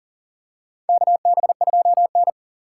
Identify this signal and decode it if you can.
Morse code, heard as a keyed tone, about 40 wpm, 700 Hz. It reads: KB2N
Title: USA Call Signs - 3x Character Spacing - 40wpm